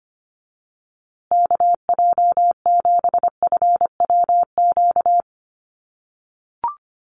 Morse code, heard as a keyed tone, about 25 wpm, 700 Hz. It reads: KJ7FWQ
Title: USA Call Signs Repeated 2x 25wpm